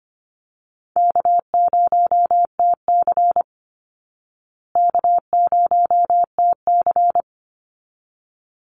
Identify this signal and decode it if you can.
Morse code, heard as a keyed tone, about 25 wpm, 700 Hz. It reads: K0TC K0TC
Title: Call Signs Repeated 3x - 25wpm